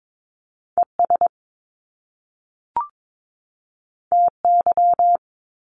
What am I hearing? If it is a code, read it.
Morse code, heard as a keyed tone, about 22 wpm, 700 Hz. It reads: ES TY
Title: Top Two Letter Combinations in Words 22wpm